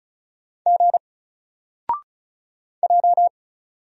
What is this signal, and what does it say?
Morse code, heard as a keyed tone, about 35 wpm, 700 Hz. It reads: G J
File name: Single-Letter-Number-35wpm